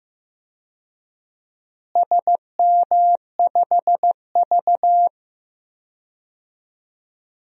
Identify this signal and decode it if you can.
Morse code, heard as a keyed tone, about 15 wpm, 700 Hz. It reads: SM5V
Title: DXCC-100 Call Signs - 4 Characters - Repeated 2x - 15wpm